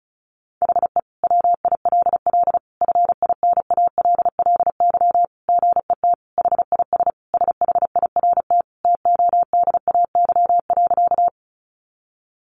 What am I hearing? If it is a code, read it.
Morse code, heard as a keyed tone, about 34 wpm, 700 Hz. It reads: HE WILL FINALLY GET HIS SHIRT TODAY.
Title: Additional Easy Phrases 35wpm